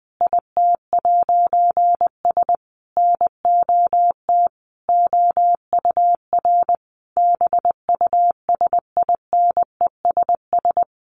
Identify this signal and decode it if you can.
Morse code, heard as a keyed tone, about 20 wpm, 700 Hz. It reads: IT'S NOT OUR BUSINESS
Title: A Sound Of Thunder 20wpm